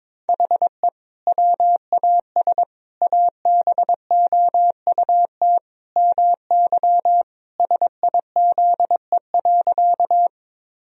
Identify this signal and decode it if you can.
Morse code, heard as a keyed tone, about 22 wpm, 700 Hz. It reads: HE WAS ABOUT MY SIZE.
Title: Sentences from Top 1000 Words - Review Part 2 - 22wpm